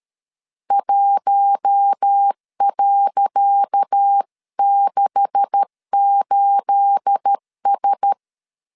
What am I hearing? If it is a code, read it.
Morse code, heard as a keyed tone, about 12 wpm, 800 Hz. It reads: 1.68S
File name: LEZIONI / 10 SIGNS / 10C DOT 12wpm 2451